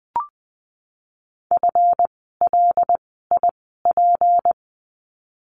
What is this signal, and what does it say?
Morse code, heard as a keyed tone, about 20 wpm, spaced as for 15 wpm, 700 Hz. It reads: FLIP
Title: 153 Lesson 25f - Letter p - Words 20wpm